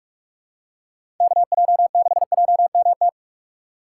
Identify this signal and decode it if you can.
Morse code, heard as a keyed tone, about 44 wpm, 700 Hz. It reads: KJ6JMT
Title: USA Call Signs - 6 Characters - 6x Character Spacing - 45wpm